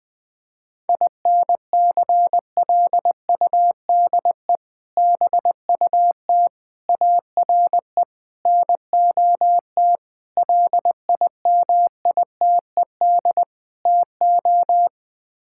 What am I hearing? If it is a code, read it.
Morse code, heard as a keyed tone, about 20 wpm, 700 Hz. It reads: INCLUDE BUT ARE NOT LIMITED TO
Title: Sets of 6 Words - 20wpm